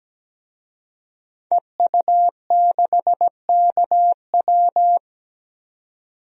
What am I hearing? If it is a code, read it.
Morse code, heard as a keyed tone, about 17 wpm, 700 Hz. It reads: EU6KW
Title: DXCC-100 Call Signs - 5 Characters - 17wpm